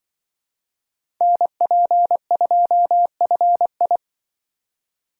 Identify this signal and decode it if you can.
Morse code, heard as a keyed tone, about 23 wpm, 700 Hz. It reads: NP2FI
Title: Speed Racing - USA 2x2 Call Signs 20wpm